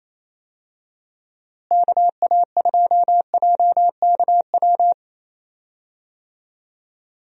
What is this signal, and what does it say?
Morse code, heard as a keyed tone, about 28 wpm, 700 Hz. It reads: KA2JKW
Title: USA Call Signs - 6 Characters - Repeated 3x - 28wpm